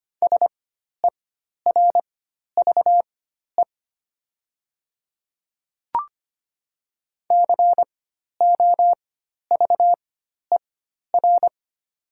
Morse code, heard as a keyed tone, about 25 wpm, spaced as for 12 wpm, 700 Hz. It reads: SERVE COVER
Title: 4x Character Spacing - Top 500 Words - 25wpm